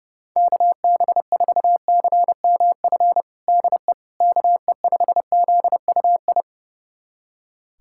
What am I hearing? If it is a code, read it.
Morse code, heard as a keyed tone, about 32 wpm, 700 Hz. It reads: KB4CMF DE KE5ZUI
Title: USA Callsign DE Callsign 30wpm